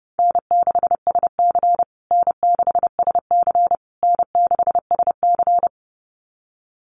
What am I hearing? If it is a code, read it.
Morse code, heard as a keyed tone, about 30 wpm, 700 Hz. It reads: N6SC N6SC N6SC
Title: USA Call Signs Repeated 3x 30wpm